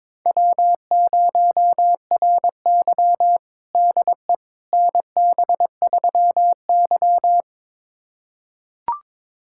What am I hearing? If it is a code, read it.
Morse code, heard as a keyed tone, about 22 wpm, 700 Hz. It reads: W0RY DE NB3Y
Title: USA Callsign DE Callsign - 4 Characters - 22wpm